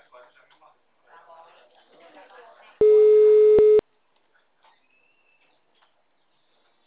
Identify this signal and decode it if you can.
Ringback tone (Europe, 425 Hz), with speech.
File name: out- -2012- - - .46024